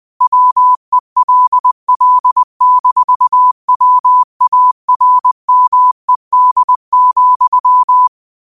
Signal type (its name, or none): Morse code